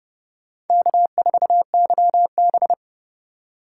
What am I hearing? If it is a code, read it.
Morse code, heard as a keyed tone, about 30 wpm, 700 Hz. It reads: K4YB
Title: USA Call Signs - 2x Character Spacing - 30wpm